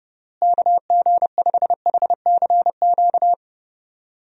Morse code, heard as a keyed tone, about 30 wpm, 700 Hz. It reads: KG5HCQ
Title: USA Call Signs - 6 Characters - Repeated 2x - 30wpm